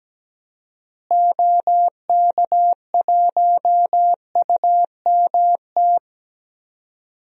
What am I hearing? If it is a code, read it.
Morse code, heard as a keyed tone, about 17 wpm, 700 Hz. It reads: OK1UMT
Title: DXCC-100 Call Signs - 6 Characters - Repeated 3x - 17wpm